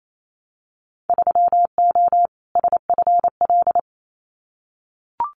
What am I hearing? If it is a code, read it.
Morse code, heard as a keyed tone, about 28 wpm, 700 Hz. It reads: 3O SFL
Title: Winter Field Day Exchanges 28wpm